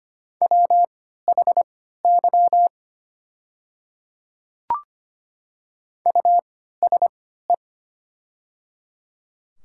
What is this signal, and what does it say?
Morse code, heard as a keyed tone, about 25 wpm, spaced as for 14 wpm, 690 Hz. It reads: WHY USE
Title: 3x Character Spacing - Top 200 Words - 25wpm